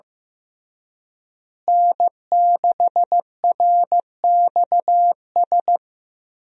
Morse code, heard as a keyed tone, about 15 wpm, 700 Hz. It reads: N6RXS
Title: USA Call Signs - 5 Characters - Repeated 3x - 15wpm